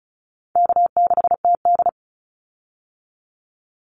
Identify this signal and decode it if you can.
Morse code, heard as a keyed tone, about 34 wpm, 700 Hz. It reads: K6TD
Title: Call Signs Repeated 3x - 35wpm